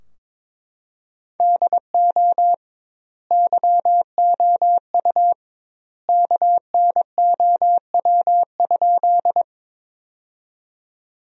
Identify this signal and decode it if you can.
Morse code, heard as a keyed tone, about 21 wpm, 700 Hz. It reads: DO YOU KNOW?
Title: 2x Word-Spacing - Sentences from Top 100 Words - 22wpm